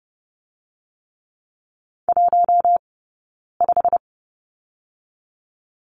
Morse code, heard as a keyed tone, about 30 wpm, 700 Hz. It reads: 1 5
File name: 092 Lesson 16d - Letter c - Two Characters 30wpm